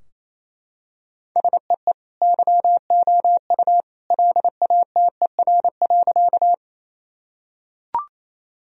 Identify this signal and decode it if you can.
Morse code, heard as a keyed tone, about 28 wpm, 700 Hz. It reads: SEE YOU LATER.
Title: Sentences from Top 1000 Words - Review Part 4 - 28wpm